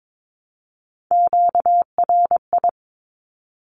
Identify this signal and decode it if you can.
Morse code, heard as a keyed tone, about 22 wpm, 700 Hz. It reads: QRI
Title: Q-Codes 22wpm